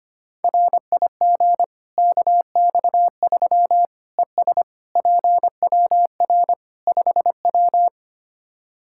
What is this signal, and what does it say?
Morse code, heard as a keyed tone, about 25 wpm, 700 Hz. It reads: RIG KX3 ES PWR 5W
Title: K1USN SST QSO Practice 25wpm